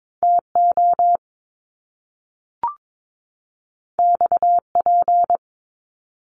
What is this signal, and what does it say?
Morse code, heard as a keyed tone, about 22 wpm, 700 Hz. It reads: TO XP
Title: Top Two Letter Combinations in Words 22wpm